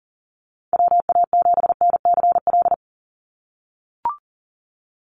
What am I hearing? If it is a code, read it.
Morse code, heard as a keyed tone, about 40 wpm, 700 Hz. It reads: WA7NCL
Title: NA SOTA Activator Callsigns 40wpm